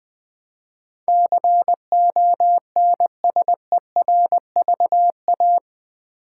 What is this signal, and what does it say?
Morse code, heard as a keyed tone, about 20 wpm, 700 Hz. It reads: CONSERVA
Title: Build A Word - 4 Letters At A Time - 12-Letter Words - 20wpm